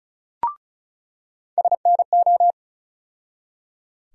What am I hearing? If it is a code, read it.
Morse code, heard as a keyed tone, about 36 wpm, 700 Hz. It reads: SNO
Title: USA Call Sign Postfixes - 3 Letters 35wpm